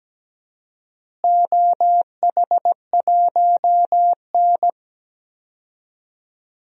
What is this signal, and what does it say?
Morse code, heard as a keyed tone, about 17 wpm, 700 Hz. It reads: OH1N
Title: DXCC-100 Call Signs - 4 Characters - Repeated 3x - 17wpm